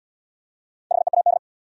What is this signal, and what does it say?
Morse code, heard as a keyed tone, about 73 wpm, 700 Hz. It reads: HEAD